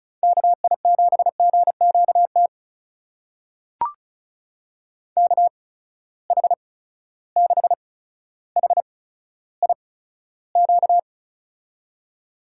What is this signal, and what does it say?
Morse code, heard as a keyed tone, about 34 wpm, spaced as for 15 wpm, 700 Hz. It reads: KI7GQT KH6HIQ
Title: USA Call Signs - 6 Characters - 8x Character Spacing - 35wpm